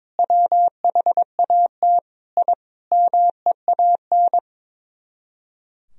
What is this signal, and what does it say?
Morse code, heard as a keyed tone, about 23 wpm, 700 Hz. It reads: WHAT I MEAN
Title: Speed-Racing - Sets of 5 Words 22wpm